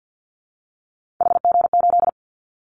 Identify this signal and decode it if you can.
Morse code, heard as a keyed tone, about 50 wpm, 700 Hz. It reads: HG8
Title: Callsign Prefixes for Top 25 DXCC - 50wpm